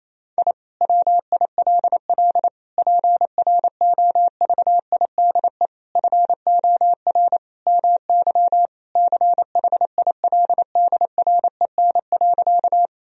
Morse code, heard as a keyed tone, about 28 wpm, 700 Hz. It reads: I WILL PROVIDE FOR MY CHILDREN.
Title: Sentences from Top 1000 Words - Review Part 3 - 28wpm